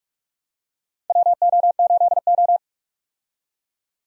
Morse code, heard as a keyed tone, about 46 wpm, 700 Hz. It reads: WO8O
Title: USA Call Signs - 4 Characters - 45wpm